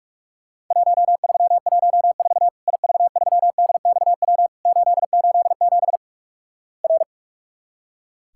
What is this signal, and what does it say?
Morse code, heard as a keyed tone, about 46 wpm, 700 Hz. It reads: 1314 IV3DXW 887 R
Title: Worked All Europe DX Contest - QTC Exchange - 45wpm